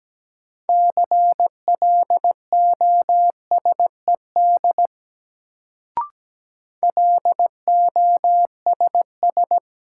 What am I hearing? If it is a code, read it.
Morse code, heard as a keyed tone, about 17 wpm, 700 Hz. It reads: CLOSED LOSS
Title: Sentences from Top 1400 Words - New Word Drill Alt - 17wpm